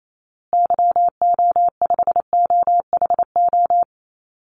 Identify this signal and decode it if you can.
Morse code, heard as a keyed tone, about 28 wpm, 700 Hz. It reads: YO5OHO
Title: Call Signs Repeated 3x - 28wpm